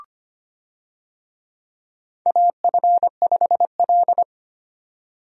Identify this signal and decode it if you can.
Morse code, heard as a keyed tone, about 25 wpm, 700 Hz. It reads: AF5L
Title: USA Call Signs - 4 Characters - 25wpm